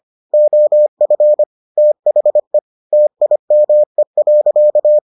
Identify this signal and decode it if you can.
Morse code, heard as a keyed tone, about 25 wpm, 600 Hz. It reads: OF THE TIME.